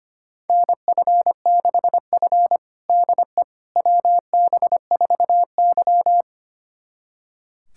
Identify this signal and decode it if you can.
Morse code, heard as a keyed tone, about 25 wpm, 700 Hz. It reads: NF6F DE WB4Y
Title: USA Callsign DE Callsign - 4 Characters - 25wpm